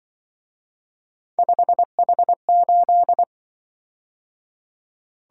Morse code, heard as a keyed tone, about 24 wpm, 700 Hz. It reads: 5H8
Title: Speed-Racing - Three Letter-Number Combinations - 20wpm